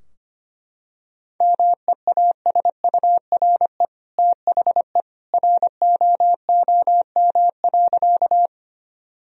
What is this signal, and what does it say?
Morse code, heard as a keyed tone, about 25 wpm, 700 Hz. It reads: MEASURE THE ROOM.
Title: Sentences from Top 1000 Words - Review Part 2 - 25wpm